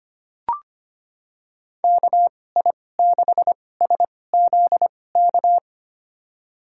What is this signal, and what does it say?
Morse code, heard as a keyed tone, about 25 wpm, spaced as for 18 wpm, 700 Hz. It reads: KI6SZK
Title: USA Call Signs - 6 Characters - 2x Character Spacing - 25wpm